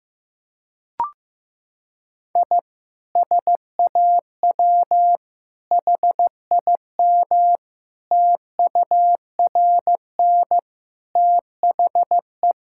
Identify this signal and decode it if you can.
Morse code, heard as a keyed tone, about 15 wpm, 700 Hz. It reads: I SAW HIM TURN THE
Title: Sentences from Top 1600 Words 15wpm